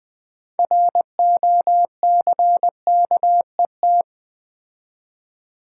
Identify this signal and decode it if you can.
Morse code, heard as a keyed tone, about 20 wpm, 700 Hz. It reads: ROCKET
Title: Build A Word - 3 Letters At A Time - 6-Letter Words - 20wpm